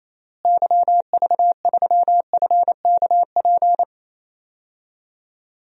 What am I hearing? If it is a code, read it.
Morse code, heard as a keyed tone, about 28 wpm, 700 Hz. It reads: YV3FKP
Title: DXCC-100 Call Signs - 6 Characters - Repeated 2x - 28wpm